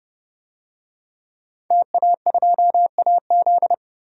Morse code, heard as a keyed tone, about 30 wpm, 700 Hz. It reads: TA2AZ
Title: DXCC-100 Call Signs - 5 Characters - 30wpm